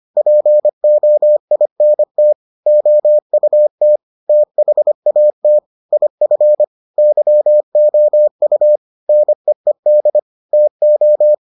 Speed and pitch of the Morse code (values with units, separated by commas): 25 wpm, 600 Hz